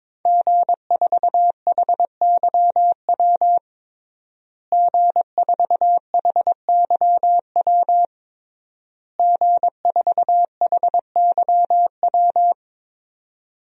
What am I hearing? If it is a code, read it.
Morse code, heard as a keyed tone, about 22 wpm, 700 Hz. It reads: G4HYW G4HYW G4HYW
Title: DXCC-100 Call Signs - 5 Characters - Repeated 3x - 22wpm